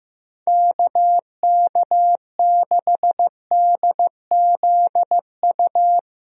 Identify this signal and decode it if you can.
Morse code, heard as a keyed tone, about 15 wpm, 700 Hz. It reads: KK6DZU
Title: USA Call Signs - 6 Characters - Repeated 3x - 15wpm